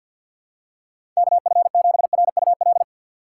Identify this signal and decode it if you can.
Morse code, heard as a keyed tone, about 53 wpm, 700 Hz. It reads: KF7RUL
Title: USA Call Signs - 6 Characters - 6x Character Spacing - 50wpm